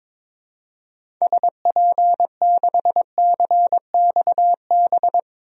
Morse code, heard as a keyed tone, about 22 wpm, 700 Hz. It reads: SP6CXB
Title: DXCC-100 Call Signs - 6 Characters - 22wpm